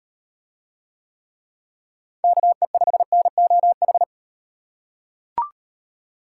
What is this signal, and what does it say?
Morse code, heard as a keyed tone, about 35 wpm, 700 Hz. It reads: KE5NOH